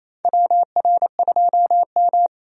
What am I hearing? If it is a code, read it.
Morse code, heard as a keyed tone, about 28 wpm, 700 Hz. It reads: WR2M
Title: USA Call Signs - 4 Characters - 28wpm